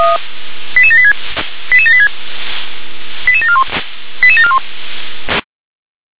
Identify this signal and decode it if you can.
Five-tone selective call (ZVEI), heard as quick runs of five tones, tones of about 70 ms. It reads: 88786 88786 88741 88741